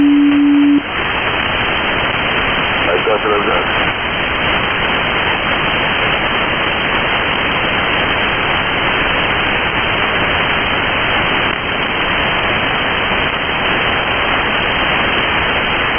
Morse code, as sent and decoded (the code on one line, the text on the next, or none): none